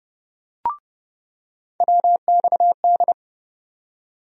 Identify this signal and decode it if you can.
Morse code, heard as a keyed tone, about 30 wpm, 700 Hz. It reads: WXD
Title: USA Call Sign Postfixes - 3 Letters 30wpm